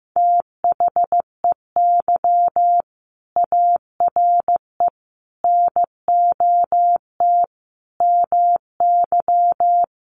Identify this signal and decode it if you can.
Morse code, heard as a keyed tone, about 15 wpm, 700 Hz. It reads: THEY ARE NOT MY
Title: Sentences from Top 1100 Words 15wpm